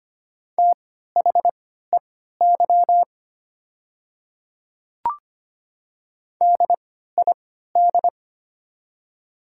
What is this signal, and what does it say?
Morse code, heard as a keyed tone, about 25 wpm, spaced as for 14 wpm, 690 Hz. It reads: THEY DID
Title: 3x Character Spacing - Top 200 Words - 25wpm